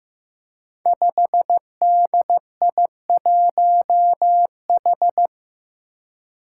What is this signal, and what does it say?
Morse code, heard as a keyed tone, about 16 wpm, 700 Hz. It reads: 5DI1H
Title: Five Challenging Letter-Number Combinations 15wpm